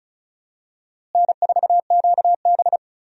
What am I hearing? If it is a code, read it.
Morse code, heard as a keyed tone, about 36 wpm, 700 Hz. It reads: N4QB
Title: USA Call Signs - 4 Characters - Repeated 2x - 35wpm